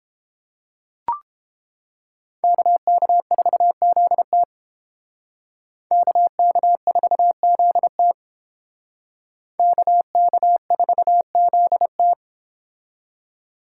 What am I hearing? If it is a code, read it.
Morse code, heard as a keyed tone, about 31 wpm, 700 Hz. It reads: KK4ZT KK4ZT KK4ZT